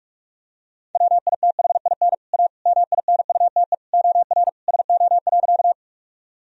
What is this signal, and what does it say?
Morse code, heard as a keyed tone, about 41 wpm, 700 Hz. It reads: WITHIN A MINUTE OR SO.
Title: Sentences from Top 1000 Words - Review Part 4 - 45wpm